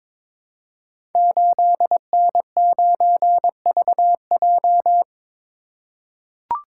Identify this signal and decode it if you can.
Morse code, heard as a keyed tone, about 22 wpm, 700 Hz. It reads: 8N9VJ
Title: Code Groups - 5 Letter:Numbers - 22wpm